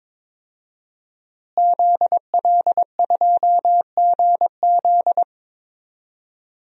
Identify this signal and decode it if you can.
Morse code, heard as a keyed tone, about 22 wpm, 700 Hz. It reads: ZL2GZ
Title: DXCC-100 Call Signs - 5 Characters - Repeated 3x - 22wpm